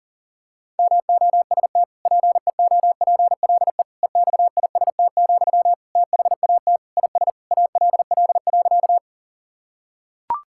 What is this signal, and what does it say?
Morse code, heard as a keyed tone, about 37 wpm, 700 Hz. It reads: MOST PEOPLE EXIST, THAT IS ALL.
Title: Sentences from Top 1000 Words - Review Part 5 - 40wpm